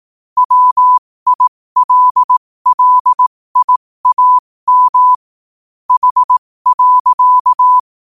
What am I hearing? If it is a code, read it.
Morse code, heard as a keyed tone, about 18 wpm, spaced as for 15 wpm, 1000 Hz. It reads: WILLIAM H.
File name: qotdfiles / 15wpm